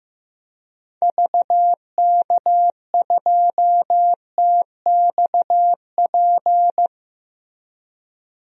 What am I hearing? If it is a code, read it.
Morse code, heard as a keyed tone, about 15 wpm, 700 Hz. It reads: VK2TXP
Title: DXCC-100 Call Signs - 6 Characters - 15wpm